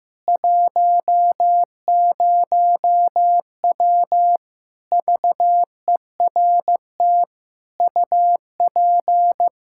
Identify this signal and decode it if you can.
Morse code, heard as a keyed tone, about 15 wpm, 700 Hz. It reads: 10W VERT UP
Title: Practice QSO - Exchange 2 - Info - 15wpm